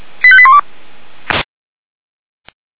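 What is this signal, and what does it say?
Five-tone selective call (ZVEI), heard as quick runs of five tones, tones of about 70 ms. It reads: 85812
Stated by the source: ZVEI five-tone call 85381 received by SDR.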